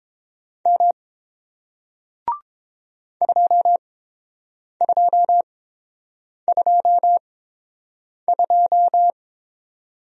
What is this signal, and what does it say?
Morse code, heard as a keyed tone, about 30 wpm, spaced as for 7 wpm, 700 Hz. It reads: M 2222